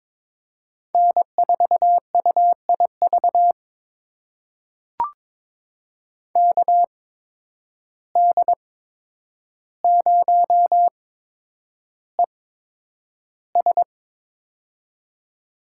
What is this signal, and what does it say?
Morse code, heard as a keyed tone, about 21 wpm, spaced as for 9 wpm, 700 Hz. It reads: N4UIV KD0ES
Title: USA Call Signs - 5 Characters - 8x Character Spacing - 22wpm